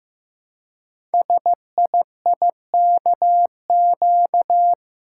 Morse code, heard as a keyed tone, about 15 wpm, 700 Hz. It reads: SIIKQ
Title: Code Groups - 5 Letters - 15wpm